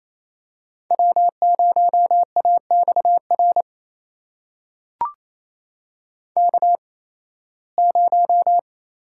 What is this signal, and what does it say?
Morse code, heard as a keyed tone, about 28 wpm, spaced as for 18 wpm, 700 Hz. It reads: W0AXR K0
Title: USA Call Signs - 5 Characters - 8x Character Spacing - 28wpm